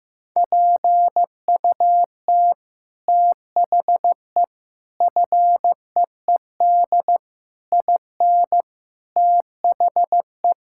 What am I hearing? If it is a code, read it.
Morse code, heard as a keyed tone, about 15 wpm, 700 Hz. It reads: PUT THE FEED IN THE
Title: Sentences from Top 1600 Words 15wpm